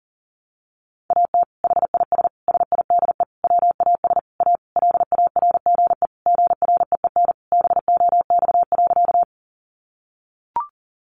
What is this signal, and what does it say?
Morse code, heard as a keyed tone, about 40 wpm, 700 Hz. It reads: AT HIS SIDE WAS A LARGE GREEN BOX.
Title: Sentences from Top 500 Words 40wpm